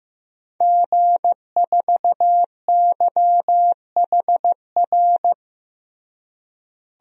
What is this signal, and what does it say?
Morse code, heard as a keyed tone, about 15 wpm, 700 Hz. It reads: G4YHR
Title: DXCC-100 Call Signs - 5 Characters - Repeated 3x - 15wpm